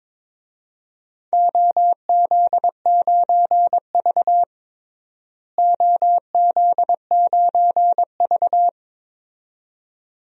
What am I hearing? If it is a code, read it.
Morse code, heard as a keyed tone, about 22 wpm, 700 Hz. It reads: OZ9V OZ9V
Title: DXCC-100 Call Signs - 4 Characters - Repeated 2x - 22wpm